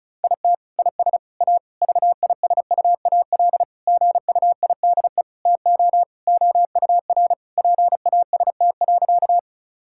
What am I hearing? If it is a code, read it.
Morse code, heard as a keyed tone, about 36 wpm, 700 Hz. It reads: IT IS A VISUAL GUIDE TO OUR PAST.
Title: Sentences from Top 1000 Words - Review Part 6 - 35wpm